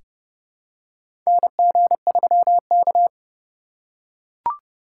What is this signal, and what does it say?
Morse code, heard as a keyed tone, about 28 wpm, 700 Hz. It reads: NG3K
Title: USA Call Signs - 4 Characters - 4x Character Spacing - 30wpm